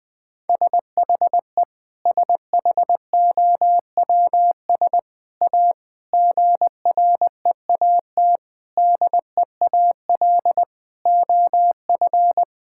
Morse code, heard as a keyed tone, about 21 wpm, 700 Hz. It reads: SHE SHOWS A GREAT DEAL OF
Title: Sentences with 4-Word Phrases - 20wpm